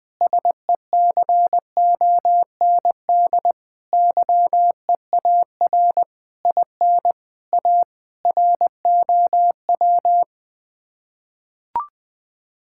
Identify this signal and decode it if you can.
Morse code, heard as a keyed tone, about 21 wpm, 700 Hz. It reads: SECOND YEAR IN A ROW
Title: Sets of 6 Words - 20wpm